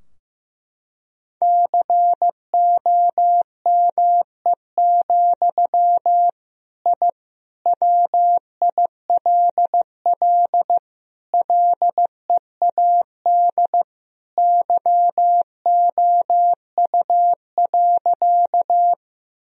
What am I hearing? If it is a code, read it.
Morse code, heard as a keyed tone, about 15 wpm, 700 Hz. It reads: COME, I WILL LEAD YOU.
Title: Sentences from Top 1000 Words - Review Part 2 - 15wpm